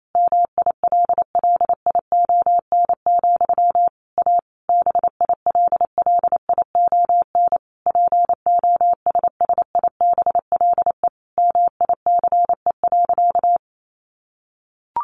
Morse code, heard as a keyed tone, about 28 wpm, 700 Hz. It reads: MILLION, A BILLION POSSIBLE MICE.
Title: A Sound Of Thunder 28wpm